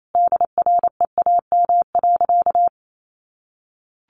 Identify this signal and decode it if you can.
Morse code, heard as a keyed tone, about 28 wpm, 700 Hz. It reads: DREAM.
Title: Sentences from Top 1500 Words 28wpm